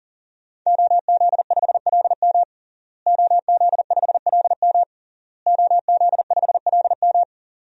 Morse code, heard as a keyed tone, about 40 wpm, 700 Hz. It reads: OZ5LM OZ5LM OZ5LM
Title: DXCC-100 Call Signs - 5 Characters - Repeated 3x - 40wpm